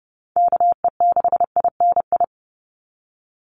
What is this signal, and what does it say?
Morse code, heard as a keyed tone, about 30 wpm, 700 Hz. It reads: KE6INI
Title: USA Call Signs 30wpm